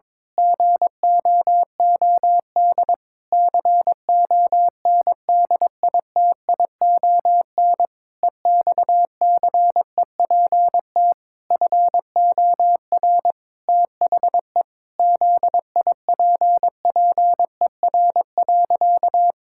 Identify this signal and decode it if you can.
Morse code, heard as a keyed tone, about 22 wpm, 700 Hz. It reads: GOOD CONDITION EXCEPT FOR THE ZIPPER.
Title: Pangram Challenge v2 - 22wpm